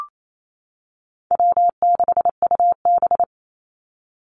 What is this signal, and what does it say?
Morse code, heard as a keyed tone, about 28 wpm, 700 Hz. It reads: W6UB
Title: NA SOTA Chaser Callsigns 28wpm